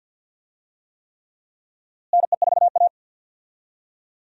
Morse code, heard as a keyed tone, about 52 wpm, 700 Hz. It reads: NE4A
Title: USA Call Signs - 4 Characters - Repeated 3x - 50wpm